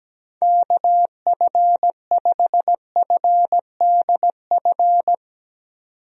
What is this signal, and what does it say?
Morse code, heard as a keyed tone, about 17 wpm, 700 Hz. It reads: KF5FDF
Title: USA Call Signs - 6 Characters - 17wpm